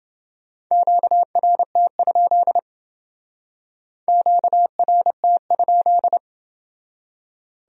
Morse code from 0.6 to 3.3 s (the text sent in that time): QRT?